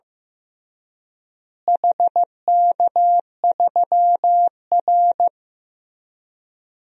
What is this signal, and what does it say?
Morse code, heard as a keyed tone, about 15 wpm, 700 Hz. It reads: HK3R
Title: DXCC-100 Call Signs - 4 Characters - Repeated 3x - 15wpm